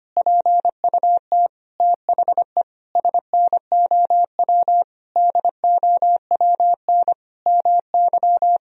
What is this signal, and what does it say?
Morse code, heard as a keyed tone, about 25 wpm, 700 Hz. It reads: PUT THE SNOW DOWN MY
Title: Sentences from Top 1000 Words - Review Part 3 - 25wpm